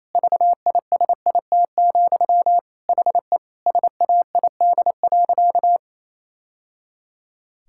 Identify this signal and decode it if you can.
Morse code, heard as a keyed tone, about 27 wpm, 700 Hz. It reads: VISIT, HE SAID.
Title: Sentences from Top 1000 Words - Review Part 6 - 28wpm